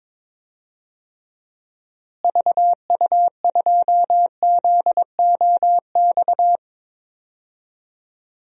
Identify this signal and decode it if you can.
Morse code, heard as a keyed tone, about 22 wpm, 700 Hz. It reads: VU2ZOX
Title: DXCC-100 Call Signs - 6 Characters - Repeated 2x - 22wpm